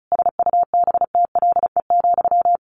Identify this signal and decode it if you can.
Morse code, heard as a keyed tone, about 34 wpm, 700 Hz. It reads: SUBTLE,
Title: A Sound Of Thunder 35wpm